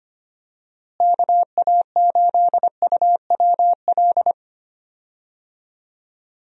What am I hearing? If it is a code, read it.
Morse code, heard as a keyed tone, about 25 wpm, 700 Hz. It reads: KA8UWL
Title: USA Call Signs - 6 Characters - Repeated 3x - 25wpm